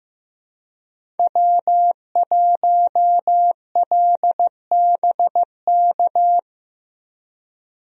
Morse code, heard as a keyed tone, about 15 wpm, 700 Hz. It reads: W1LBK
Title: USA Call Signs - 5 Characters - Repeated 3x - 15wpm